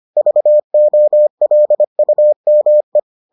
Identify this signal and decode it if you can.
Morse code, heard as a keyed tone, about 25 wpm, 600 Hz. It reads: VOLUME